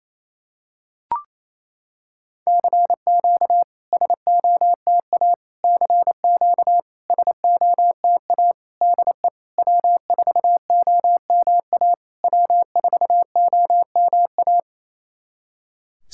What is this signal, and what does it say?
Morse code, heard as a keyed tone, about 28 wpm, 700 Hz. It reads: CQ SOTA CQ SOTA DE W4OMA W4OMA
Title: SOTA QSO Practice 28wpm